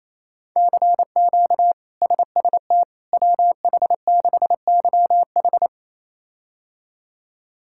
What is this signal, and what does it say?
Morse code, heard as a keyed tone, about 28 wpm, 700 Hz. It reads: CQ SST WH6YH
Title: K1USN SST QSO Practice - 28wpm